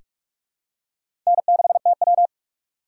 Morse code, heard as a keyed tone, about 43 wpm, 700 Hz. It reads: N6TW
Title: K1USN SST QSO Practice - 45wpm